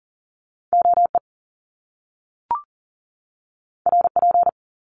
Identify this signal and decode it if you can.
Morse code, heard as a keyed tone, about 41 wpm, 700 Hz. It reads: OE RP
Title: Top Two Letter Combinations in Words 40wpm